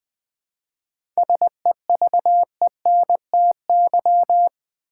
Morse code, heard as a keyed tone, about 20 wpm, 700 Hz. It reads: SEVENTY
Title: 7 Letter Words - First 3 Letters Given - 20wpm